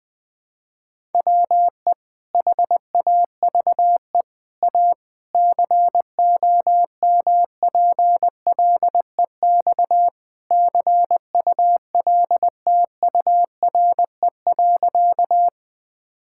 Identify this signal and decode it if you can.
Morse code, heard as a keyed tone, about 20 wpm, 700 Hz. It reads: WE HAVE A COMPLEX CULTURE.
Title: Sentences from Top 1000 Words - Review Part 6 - 20wpm